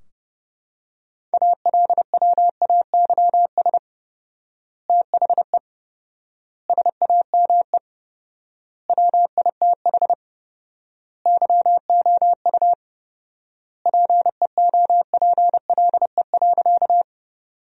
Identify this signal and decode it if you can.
Morse code, heard as a keyed tone, about 30 wpm, 700 Hz. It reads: ALWAYS THE SAME WITH YOU PEOPLE.
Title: 4x Word-Spacing - Sentences from Top 300 Words - 30wpm